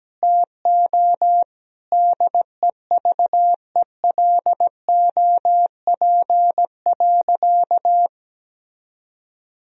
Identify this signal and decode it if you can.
Morse code, heard as a keyed tone, about 17 wpm, 700 Hz. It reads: TO DEVELOP.
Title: Sentences from Top 1000 Words - Review Part 2 - 17wpm